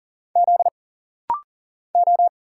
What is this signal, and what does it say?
Morse code, heard as a keyed tone, about 39 wpm, 700 Hz. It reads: Z O
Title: Single Letter-Number - Rapid-Fire 40wpm